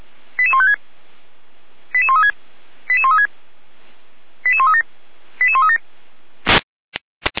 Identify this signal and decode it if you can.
Five-tone selective call (ZVEI), heard as quick runs of five tones, tones of about 70 ms. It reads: 88156 88136 88136 88137 88137